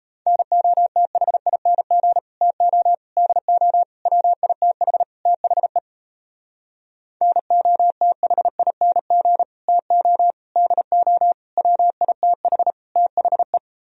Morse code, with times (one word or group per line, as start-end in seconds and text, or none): 0.3-2.2 NOTHING
2.4-2.9 TO
3.2-3.8 DO
4.1-5.0 WITH
5.3-5.8 THE
7.2-9.4 NOTHING
9.7-10.3 TO
10.6-11.3 DO
11.6-12.7 WITH
13.0-13.6 THE